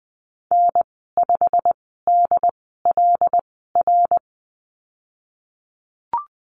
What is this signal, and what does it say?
Morse code, heard as a keyed tone, about 20 wpm, spaced as for 15 wpm, 700 Hz. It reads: N5DLR
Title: 128 Lesson 21g - Number 6 - Call Signs 20wpm